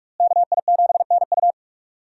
Morse code, heard as a keyed tone, about 42 wpm, 700 Hz. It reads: KI7NU
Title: USA Call Signs - 5 Characters - Repeated 3x - 45wpm